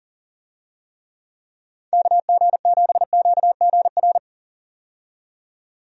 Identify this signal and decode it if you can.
Morse code, heard as a keyed tone, about 39 wpm, 700 Hz. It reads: KG7QGR
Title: USA Call Signs - 6 Characters - 40wpm